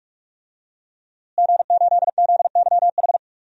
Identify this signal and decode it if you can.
Morse code, heard as a keyed tone, about 46 wpm, 700 Hz. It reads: G8ZYH
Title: DXCC-100 Call Signs - 5 Characters - Repeated 3x - 45wpm